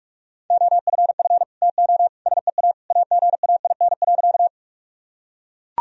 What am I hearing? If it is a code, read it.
Morse code, heard as a keyed tone, about 41 wpm, 700 Hz. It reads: OFF TO SEA AGAIN.
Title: Sentences from Top 1000 Words - Review Part 1 - 45wpm